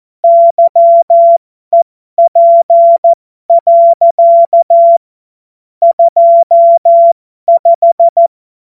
Morse code, heard as a keyed tone, about 14 wpm, spaced as for 12 wpm, 680 Hz. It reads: YEP. 25